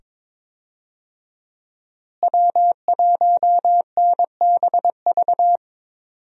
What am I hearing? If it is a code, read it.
Morse code, heard as a keyed tone, about 22 wpm, 700 Hz. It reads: W1NBV
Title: USA Call Signs - 5 Characters - Repeated 2x - 22wpm